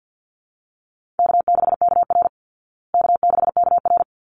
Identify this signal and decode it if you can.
Morse code, heard as a keyed tone, about 50 wpm, 700 Hz. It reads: K6KR K6KR
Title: Easy Call Signs Repeated 3x - 50wpm